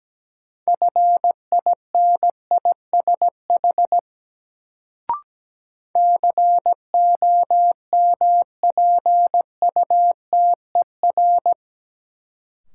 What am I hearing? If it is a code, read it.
Morse code, heard as a keyed tone, about 17 wpm, 700 Hz. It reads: FINISH COMPUTER
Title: Sentences from Top 1000 Words - Review Part 3 - New Word Drill - 17wpm